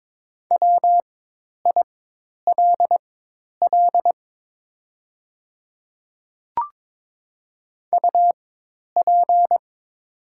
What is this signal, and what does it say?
Morse code, heard as a keyed tone, about 22 wpm, spaced as for 10 wpm, 700 Hz. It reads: WILL UP
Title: 4x Character Spacing - Top 200 Words - 22wpm